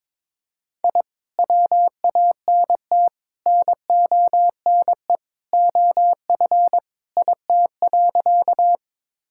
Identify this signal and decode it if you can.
Morse code, heard as a keyed tone, about 21 wpm, 700 Hz. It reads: I WANT NONE OF IT.
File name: practice / 22wpm / Sentences with 2-Word Phrases - 22wpm